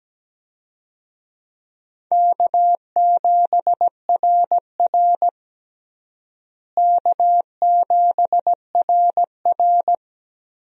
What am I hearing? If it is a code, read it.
Morse code, heard as a keyed tone, about 17 wpm, 700 Hz. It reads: K7RR K7RR
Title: USA Call Signs - 4 Characters - Repeated 2x - 17wpm